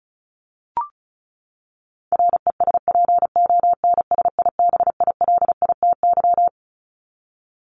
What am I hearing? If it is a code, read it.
Morse code, heard as a keyed tone, about 34 wpm, 700 Hz. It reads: RESPONSIBILITY
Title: Top 1001-1250 Words 35wpm